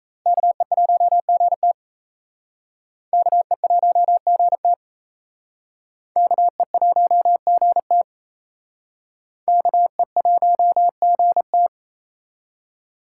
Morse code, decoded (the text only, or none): KE1GT KE1GT KE1GT KE1GT